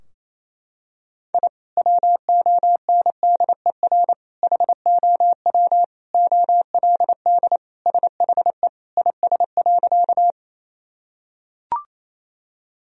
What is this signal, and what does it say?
Morse code, heard as a keyed tone, about 28 wpm, 700 Hz. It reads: I WONDER HOW OLD SHE IS.
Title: Sentences from Top 1000 Words - Review Part 3 - 28wpm